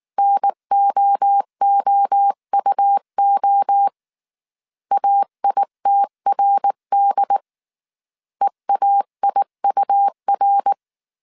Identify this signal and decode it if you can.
Morse code, heard as a keyed tone, about 19 wpm, 800 Hz. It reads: NOOUO AITRD EAIUR